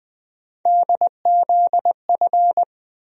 Morse code, heard as a keyed tone, about 20 wpm, 700 Hz. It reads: DZF
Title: USA Call Sign Postfixes - 3 Letters 20wpm